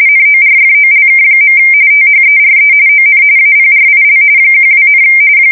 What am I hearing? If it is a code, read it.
RTTY, about 45.45 baud. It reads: NORFOLK. THERE ON APRIL 12,
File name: fsk170 / USS-New-Jersey